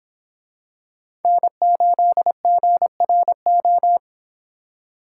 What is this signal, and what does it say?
Morse code, heard as a keyed tone, about 26 wpm, 700 Hz. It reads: N8GRO